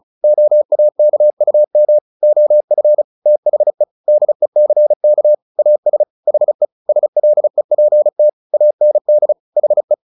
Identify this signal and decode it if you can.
Morse code, heard as a keyed tone, about 35 wpm, 600 Hz. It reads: OAKUM OF THE DECK AS HE SLEPT AND HE
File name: the old man and the sea / the old man and the sea 1 35wpm